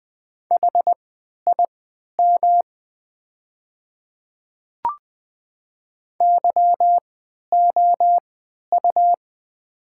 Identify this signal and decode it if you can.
Morse code, heard as a keyed tone, about 20 wpm, spaced as for 11 wpm, 700 Hz. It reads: HIM YOU
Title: 3x Character Spacing - Top 200 Words - 20wpm